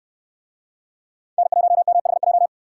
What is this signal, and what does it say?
Morse code, heard as a keyed tone, about 64 wpm, 700 Hz. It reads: N1MHJ